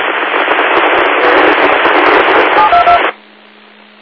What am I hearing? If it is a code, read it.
DTMF, heard as telephone touch tones, tones of about 110 ms. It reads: 722
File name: Old Forge Group 22 25 30